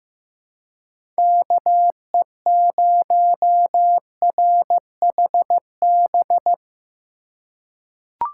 Morse code, heard as a keyed tone, about 15 wpm, 700 Hz. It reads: KE0RHB
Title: USA Call Signs - 2x Character Spacing - 15wpm